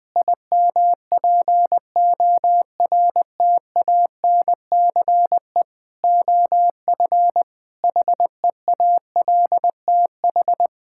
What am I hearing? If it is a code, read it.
Morse code, heard as a keyed tone, about 20 wpm, 700 Hz. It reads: IMPORTANCE OF HEALTH
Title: Sentences with 3-Word Phrases - 20wpm